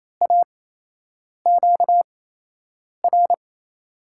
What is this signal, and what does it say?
Morse code, heard as a keyed tone, about 28 wpm, spaced as for 8 wpm, 700 Hz. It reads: AQR
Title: USA Call Signs - 8x Character Spacing - 28wpm